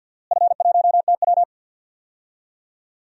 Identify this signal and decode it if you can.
Morse code, heard as a keyed tone, about 51 wpm, 700 Hz. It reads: F1TW
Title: DXCC-100 Call Signs - 4 Characters - Repeated 3x - 50wpm